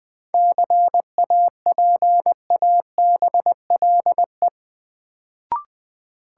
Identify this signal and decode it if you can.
Morse code, heard as a keyed tone, about 20 wpm, 700 Hz. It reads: CAPABLE
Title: Sentences from Top 1200 Words - New Word Drill Alt - 20wpm